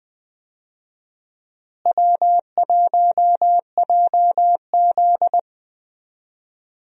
Morse code, heard as a keyed tone, about 20 wpm, 700 Hz. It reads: W1JZ
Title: USA Call Signs - 4 Characters - Repeated 3x - 20wpm